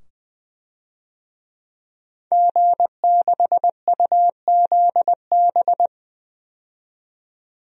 Morse code, heard as a keyed tone, about 20 wpm, 700 Hz. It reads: G6UZB
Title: DXCC-100 Call Signs - 5 Characters - 20wpm